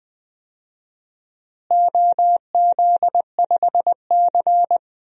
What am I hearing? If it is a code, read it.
Morse code, heard as a keyed tone, about 20 wpm, 700 Hz. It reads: OZ5C
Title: DXCC-100 Call Signs - 4 Characters - Repeated 2x - 20wpm